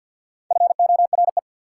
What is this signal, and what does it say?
Morse code, heard as a keyed tone, about 48 wpm, 690 Hz. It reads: FORE
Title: Prefixes and Suffixes 50wpm